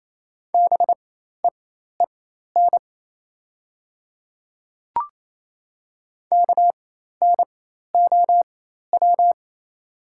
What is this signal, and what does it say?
Morse code, heard as a keyed tone, about 28 wpm, spaced as for 13 wpm, 700 Hz. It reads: BEEN KNOW
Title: 4x Character Spacing - Top 100 Words - 28wpm